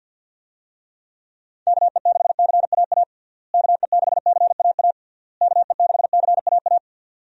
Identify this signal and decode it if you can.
Morse code, heard as a keyed tone, about 51 wpm, 700 Hz. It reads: KE6CAA KE6CAA KE6CAA
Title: USA Call Signs - 6 Characters - Repeated 3x - 50wpm